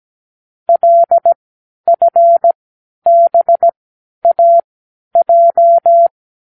Morse code, heard as a keyed tone, about 17 wpm, spaced as for 10 wpm, 680 Hz. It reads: LFBAJ
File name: k7qo / Lesson18(A-L)